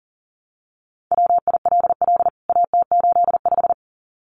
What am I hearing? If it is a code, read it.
Morse code, heard as a keyed tone, about 40 wpm, 700 Hz. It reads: WILL AT85
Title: CWT Exchange 40wpm